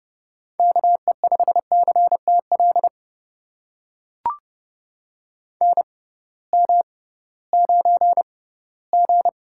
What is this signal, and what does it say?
Morse code, heard as a keyed tone, about 30 wpm, spaced as for 18 wpm, 700 Hz. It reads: KE5CTL NM9G
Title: USA Call Signs - 6x Character Spacing - 30wpm